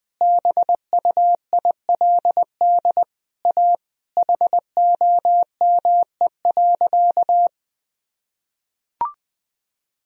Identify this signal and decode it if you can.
Morse code, heard as a keyed tone, about 20 wpm, 700 Hz. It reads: BUILD A HOME.
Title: Sentences from Top 1000 Words - Review Part 1 - 20wpm